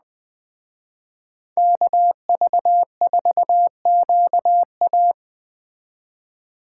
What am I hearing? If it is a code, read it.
Morse code, heard as a keyed tone, about 20 wpm, 700 Hz. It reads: KV4QA